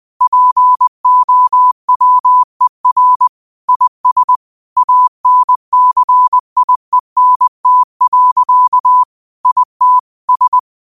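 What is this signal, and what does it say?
Morse code, heard as a keyed tone, about 20 wpm, 1000 Hz. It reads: POWER IS ANCIENT. IT S
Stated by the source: Morse code practice at 20 wpm: famous quotations sent in telegraphy, from URVAG.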